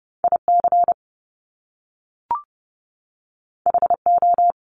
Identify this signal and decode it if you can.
Morse code, heard as a keyed tone, about 30 wpm, 700 Hz. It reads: IC HO
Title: Top Two Letter Combinations in Words 30wpm